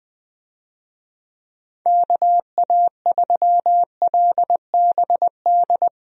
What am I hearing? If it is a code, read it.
Morse code, heard as a keyed tone, about 20 wpm, 700 Hz. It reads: KA3LBD
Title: USA Call Signs - 6 Characters - Repeated 2x - 20wpm